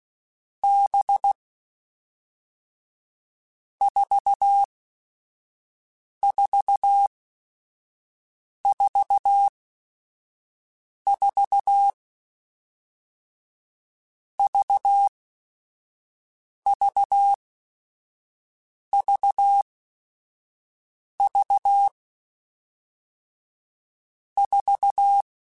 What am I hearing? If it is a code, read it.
Morse code, heard as a keyed tone, about 16 wpm, 800 Hz. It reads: B 4 4 4 4 V V V V 4
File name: Groups@5 / Groups@5 mp3s / VB46@5